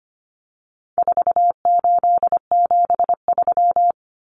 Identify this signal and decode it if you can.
Morse code, heard as a keyed tone, about 25 wpm, 700 Hz. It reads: 4873
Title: Common Cut Numbers - 4-Digit 25wpm